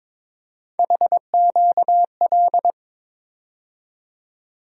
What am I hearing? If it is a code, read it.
Morse code, heard as a keyed tone, about 22 wpm, 700 Hz. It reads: HQL
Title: USA Call Sign Postfixes - 3 Letters 22wpm